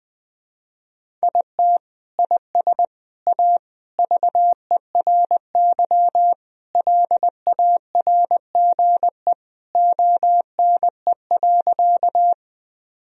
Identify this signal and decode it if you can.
Morse code, heard as a keyed tone, about 20 wpm, 700 Hz. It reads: IT IS A VERY LARGE ONE.
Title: Sentences from Top 1000 Words - Review Part 1 - 20wpm